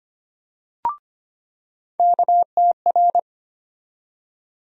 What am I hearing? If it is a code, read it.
Morse code, heard as a keyed tone, about 25 wpm, 700 Hz. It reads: KTR
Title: USA Call Sign Postfixes - 3 Letters 25wpm